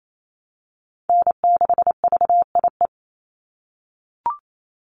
Morse code, heard as a keyed tone, about 28 wpm, 700 Hz. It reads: N6VIE
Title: USA Call Signs 28wpm